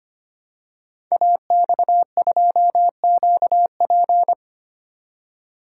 Morse code, heard as a keyed tone, about 25 wpm, 700 Hz. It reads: AX2QP
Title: Code Groups - 5 Letter:Numbers - 25wpm